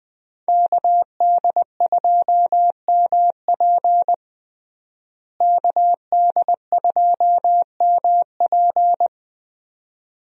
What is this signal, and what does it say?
Morse code, heard as a keyed tone, about 20 wpm, 700 Hz. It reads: KD2MP KD2MP
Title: USA Call Signs - 5 Characters - Repeated 2x - 20wpm